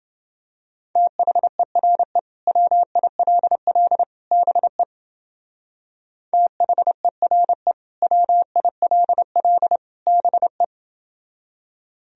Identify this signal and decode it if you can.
Morse code, heard as a keyed tone, about 30 wpm, 700 Hz. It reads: THERE WILL BE THERE WILL BE